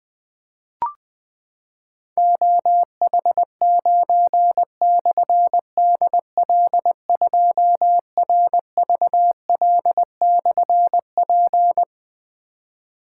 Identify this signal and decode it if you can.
Morse code, heard as a keyed tone, about 20 wpm, 700 Hz. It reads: OH9/DL2RVL/P
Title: Difficult Call Signs 20wpm